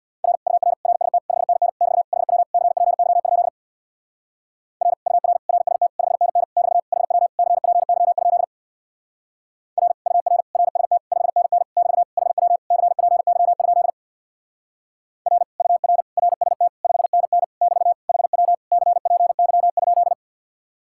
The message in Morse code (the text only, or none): R UR RST 5NN <BT> HW CPY? R UR RST 5NN <BT> HW CPY? R UR RST 5NN <BT> HW CPY? R UR RST 5NN <BT> HW CPY?